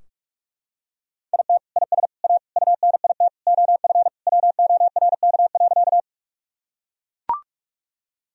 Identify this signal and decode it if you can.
Morse code, heard as a keyed tone, about 41 wpm, 700 Hz. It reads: IT IS A UNIT OF WORK.
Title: Sentences from Top 1000 Words - Review Part 2 - 45wpm